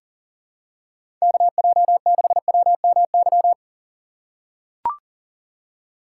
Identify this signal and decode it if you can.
Morse code, heard as a keyed tone, about 43 wpm, 700 Hz. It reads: KJ6WMY
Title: USA Call Signs - 6 Characters - 6x Character Spacing - 40wpm